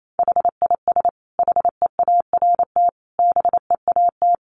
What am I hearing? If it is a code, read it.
Morse code, heard as a keyed tone, about 28 wpm, 700 Hz. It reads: HIS HEART BEAT
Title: Sentences from Top 1300 Words 28wpm